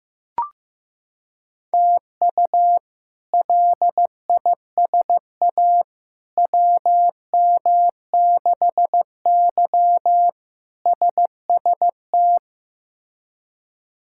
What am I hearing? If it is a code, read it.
Morse code, heard as a keyed tone, about 15 wpm, 700 Hz. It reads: TU LISA WM6Y SST
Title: K1USN SST QSO Practice - 15wpm